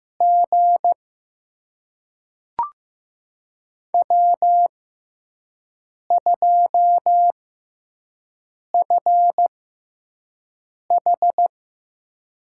Morse code, heard as a keyed tone, about 15 wpm, spaced as for 5 wpm, 700 Hz. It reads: G W2FH
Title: USA Call Signs - 5 Characters - 6x Character Spacing - 15wpm